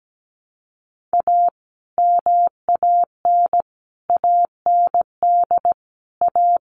Morse code, heard as a keyed tone, about 17 wpm, 700 Hz. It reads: A MAN AND A
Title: Sets of 5 Words 17wpm